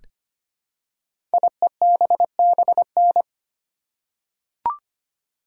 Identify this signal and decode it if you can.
Morse code, heard as a keyed tone, about 25 wpm, 700 Hz. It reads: IEBBN
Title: Code Groups - 5 Letters - 25wpm